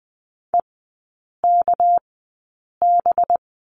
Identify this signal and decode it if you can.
Morse code, heard as a keyed tone, about 20 wpm, spaced as for 8 wpm, 700 Hz. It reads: EKB
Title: 198 Lesson 32e - Letter k - Three Characters 20wpm